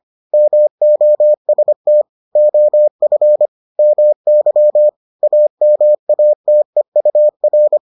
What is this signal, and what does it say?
Morse code, heard as a keyed tone, about 25 wpm, 600 Hz. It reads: MOST OF MY AMATEUR